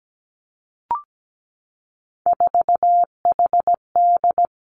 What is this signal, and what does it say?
Morse code, heard as a keyed tone, about 17 wpm, 700 Hz. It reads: 4HD
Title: Three Challenging Letter-Number Combinations 17wpm